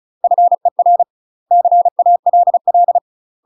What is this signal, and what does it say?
Morse code, heard as a keyed tone, about 34 wpm, 700 Hz. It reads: FER CALL
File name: UK MASTER QSOs TRAINING / 35WPM / UK MASTER QSOs 0004